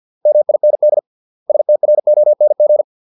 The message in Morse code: MIND STRONG

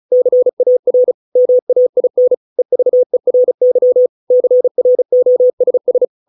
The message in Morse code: CAR MAIN EVERY CROSS

